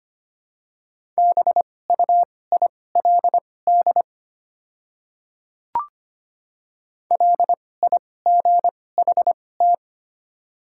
Morse code, heard as a keyed tone, about 25 wpm, spaced as for 18 wpm, 690 Hz. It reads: BUILD LIGHT
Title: 2x Character Spacing - Top 200 Words - 25wpm